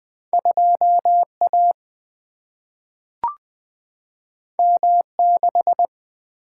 Morse code, heard as a keyed tone, about 20 wpm, 700 Hz. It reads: 2A M6
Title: Single-Letter Single-Number 20wpm